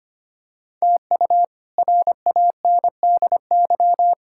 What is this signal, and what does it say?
Morse code, heard as a keyed tone, about 25 wpm, 700 Hz. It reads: TU RANDY
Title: K1USN SST QSO Practice - 25wpm